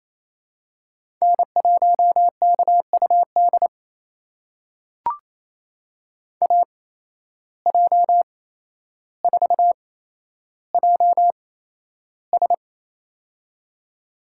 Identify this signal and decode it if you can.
Morse code, heard as a keyed tone, about 28 wpm, spaced as for 12 wpm, 700 Hz. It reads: N1KUD AJ4JS
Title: USA Call Signs - 5 Characters - 8x Character Spacing - 28wpm